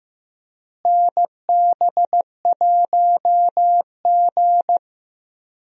Morse code, heard as a keyed tone, about 15 wpm, 700 Hz. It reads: NB1G
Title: USA Call Signs - 4 Characters - Repeated 3x - 15wpm